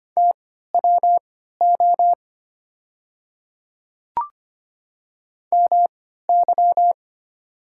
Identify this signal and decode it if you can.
Morse code, heard as a keyed tone, about 25 wpm, spaced as for 14 wpm, 700 Hz. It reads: TWO MY
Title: 3x Character Spacing - Top 100 Words - 25wpm